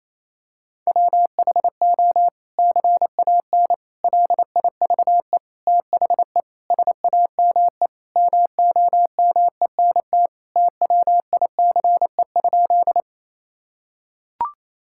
Morse code, heard as a keyed tone, about 28 wpm, 700 Hz. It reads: WHO CAN LIVE THE SAME MOMENT TWICE?
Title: Sentences from Top 1600 Words 28wpm